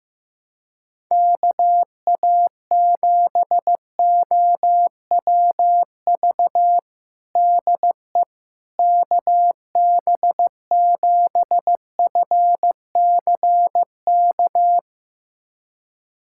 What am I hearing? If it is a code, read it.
Morse code, heard as a keyed tone, about 15 wpm, 700 Hz. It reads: KA7OWV DE KB7FCK
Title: USA Callsign DE Callsign - 6 Characters - 15wpm